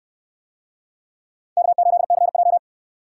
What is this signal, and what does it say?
Morse code, heard as a keyed tone, about 67 wpm, 700 Hz. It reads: K8CJ